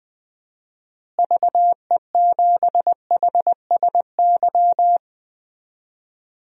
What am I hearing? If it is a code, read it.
Morse code, heard as a keyed tone, about 20 wpm, 700 Hz. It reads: VE7HSY
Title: DXCC-100 Call Signs - 6 Characters - Repeated 2x - 20wpm